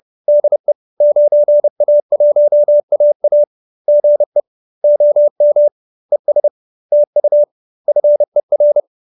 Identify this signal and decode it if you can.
Morse code, heard as a keyed tone, about 30 wpm, 590 Hz. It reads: DE 9A1AA GE OM ES TU FER
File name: QSO319 30